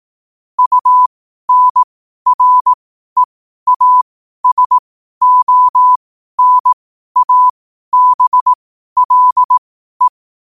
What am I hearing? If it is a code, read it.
Morse code, heard as a keyed tone, about 18 wpm, spaced as for 12 wpm, 1000 Hz. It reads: UNREASONABLE
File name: qotdfiles / 12wpm